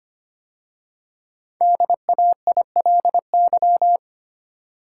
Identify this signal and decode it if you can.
Morse code, heard as a keyed tone, about 25 wpm, 700 Hz. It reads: DAILY
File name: 5 Letter Words - First 3 Letters Given - 25wpm